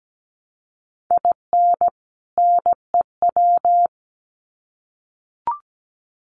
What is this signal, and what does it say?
Morse code, heard as a keyed tone, about 17 wpm, 700 Hz. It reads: IN NEW
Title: Sets of 2 Words 17wpm